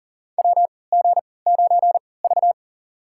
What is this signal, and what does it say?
Morse code, heard as a keyed tone, about 44 wpm, spaced as for 23 wpm, 700 Hz. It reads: WG9V
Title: USA Call Signs - 3x Character Spacing - 40wpm